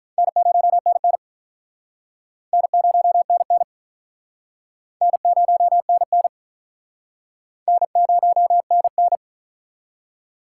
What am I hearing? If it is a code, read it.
Morse code, heard as a keyed tone, about 50 wpm, 700 Hz. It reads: N0NN N0NN N0NN N0NN